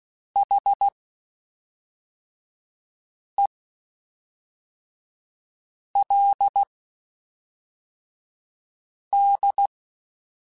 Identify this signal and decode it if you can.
Morse code, heard as a keyed tone, about 16 wpm, spaced as for 3 wpm, 800 Hz. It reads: HELD
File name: (spaced) / QSOs Spaced-mp3s / kc7sac @5 sp